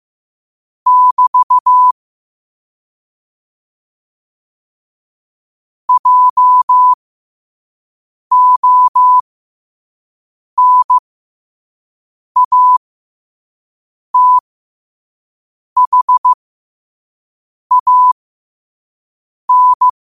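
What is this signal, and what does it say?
Morse code, heard as a keyed tone, about 15 wpm, spaced as for 5 wpm, 1000 Hz. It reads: <BT> JONATHAN
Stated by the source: Morse code at 5 wpm.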